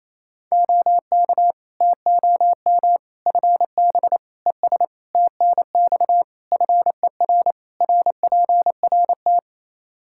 Morse code, heard as a keyed tone, about 28 wpm, 700 Hz. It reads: OK TOM FB ES TNX FER RPRT
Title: Practice QSO - Exchange 2 - Info - 28wpm